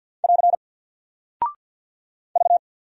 Morse code, heard as a keyed tone, about 48 wpm, 700 Hz. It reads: <AR> V
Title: Single Character (Letter:Number:Punctuation:Prosign) Alt 50wpm